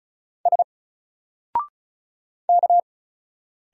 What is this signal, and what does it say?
Morse code, heard as a keyed tone, about 34 wpm, 700 Hz. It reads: S K
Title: Single Letter-Number Alt 35wpm